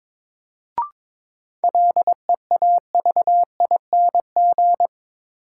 Morse code, heard as a keyed tone, about 21 wpm, 700 Hz. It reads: LEAVING
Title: Sentences from Top 1000 Words - Review Part 4 - New Word Drill - 22wpm